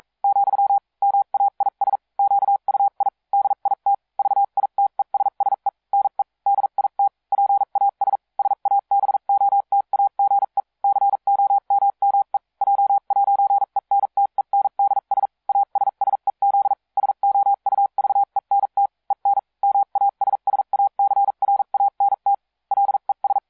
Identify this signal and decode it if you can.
Morse code, heard as a keyed tone, about 42 wpm, spaced as for 36 wpm, 800 Hz. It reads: , MAIS QUI DIT VITESSE NE DIT PAS SABOTAGE COMME J'ENTENDS ASSEZ SOUVENT EN MASSACRANT LES